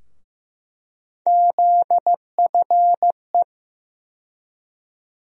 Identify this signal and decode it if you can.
Morse code, heard as a keyed tone, about 15 wpm, 700 Hz. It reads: ZFE
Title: USA Call Sign Postfixes - 3 Letters 15wpm